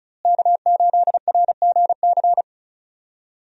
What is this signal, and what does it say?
Morse code, heard as a keyed tone, about 34 wpm, 700 Hz. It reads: K8RGC
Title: USA Call Signs - 5 Characters - Repeated 2x - 35wpm